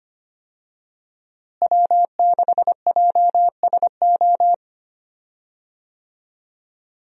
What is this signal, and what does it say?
Morse code, heard as a keyed tone, about 25 wpm, 700 Hz. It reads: W6JSO
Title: USA Call Signs - 5 Characters - Repeated 3x - 25wpm